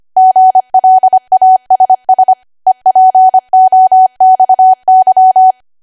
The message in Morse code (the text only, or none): GLASS EPOXY